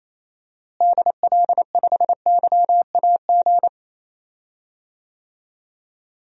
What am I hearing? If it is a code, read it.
Morse code, heard as a keyed tone, about 28 wpm, 700 Hz. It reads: DL5YAG
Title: DXCC-100 Call Signs - 6 Characters - Repeated 3x - 28wpm